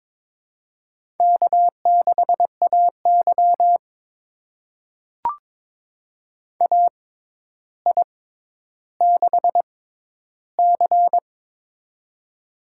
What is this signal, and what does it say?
Morse code, heard as a keyed tone, about 22 wpm, spaced as for 11 wpm, 700 Hz. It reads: K6AY AI6C
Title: USA Call Signs - 6x Character Spacing - 22wpm